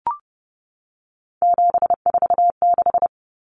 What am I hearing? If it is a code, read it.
Morse code, heard as a keyed tone, about 30 wpm, 700 Hz. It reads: 746
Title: Common Cut Numbers - 3-Digit 30wpm